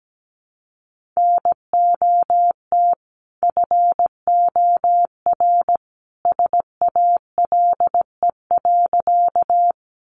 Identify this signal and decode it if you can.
Morse code, heard as a keyed tone, about 17 wpm, 700 Hz. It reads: NOT FOR SALE.
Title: Sentences from Top 1400 Words 17wpm